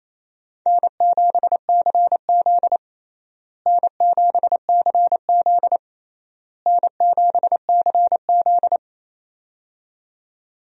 Morse code, heard as a keyed tone, about 28 wpm, 700 Hz. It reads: N7CZ N7CZ N7CZ
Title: USA Call Signs - 4 Characters - Repeated 3x - 28wpm